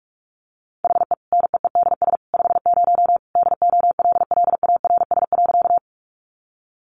Morse code, heard as a keyed tone, about 44 wpm, 700 Hz. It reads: HE NEEDS 50 DOLLARS.
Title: Additional Easy Phrases 45wpm